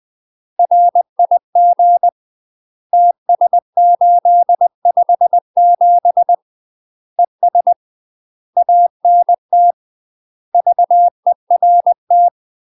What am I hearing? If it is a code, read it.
Morse code, heard as a keyed tone, about 20 wpm, 700 Hz. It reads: RIG TS857 ES ANT VERT
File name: MASTER QSOs TRAINING/20WPM